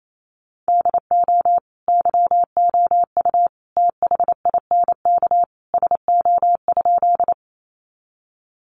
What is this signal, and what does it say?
Morse code, heard as a keyed tone, about 28 wpm, 700 Hz. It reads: DO YOU THINK SO?
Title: Sentences from Top 200 Words 28wpm